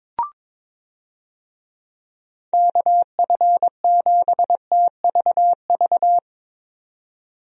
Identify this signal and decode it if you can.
Morse code, heard as a keyed tone, about 22 wpm, 700 Hz. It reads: KF7TVV
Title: USA Call Signs - 6 Characters - 22wpm